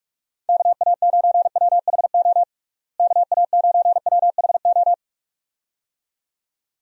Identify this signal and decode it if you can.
Morse code, heard as a keyed tone, about 45 wpm, 700 Hz. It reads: KA9WHO KA9WHO
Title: USA Call Signs - 6 Characters - Repeated 2x - 45wpm